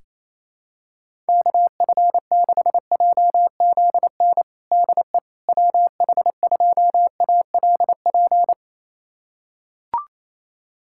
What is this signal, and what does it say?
Morse code, heard as a keyed tone, about 28 wpm, 700 Hz. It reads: KF6JZN DE WH2ALP
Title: USA Callsign DE Callsign - 6 Characters - 28wpm